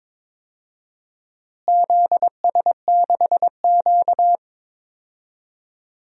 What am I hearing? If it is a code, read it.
Morse code, heard as a keyed tone, about 22 wpm, 700 Hz. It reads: ZS6Q
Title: DXCC-100 Call Signs - 4 Characters - Repeated 2x - 22wpm